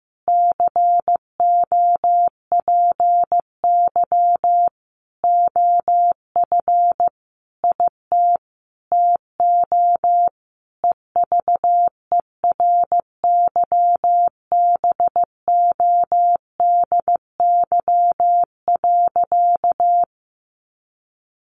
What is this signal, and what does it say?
Morse code, heard as a keyed tone, about 15 wpm, 700 Hz. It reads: COPY OF IT TO EVERYBODY.
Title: Sentences from Top 1400 Words 15wpm